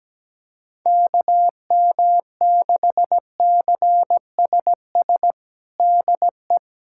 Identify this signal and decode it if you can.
Morse code, heard as a keyed tone, about 17 wpm, 700 Hz. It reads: KM6CSS DE
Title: USA Callsign DE Callsign 17wpm